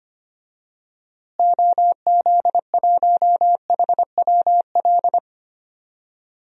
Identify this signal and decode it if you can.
Morse code, heard as a keyed tone, about 25 wpm, 700 Hz. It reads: OZ1HWL
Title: DXCC-100 Call Signs - 6 Characters - Repeated 3x - 25wpm